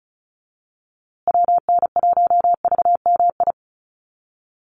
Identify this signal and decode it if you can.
Morse code, heard as a keyed tone, about 34 wpm, 700 Hz. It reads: WN1VMI
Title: USA Call Signs Repeated 3x 35wpm